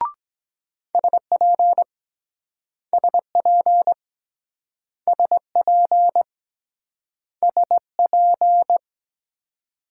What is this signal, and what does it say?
Morse code, heard as a keyed tone, about 23 wpm, 700 Hz. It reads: SP SP SP SP